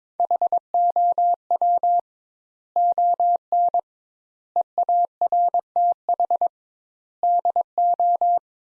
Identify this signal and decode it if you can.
Morse code, heard as a keyed tone, about 22 wpm, 700 Hz. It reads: HOW ON EARTH DO
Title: Sentences Top 200 Words - 22wpm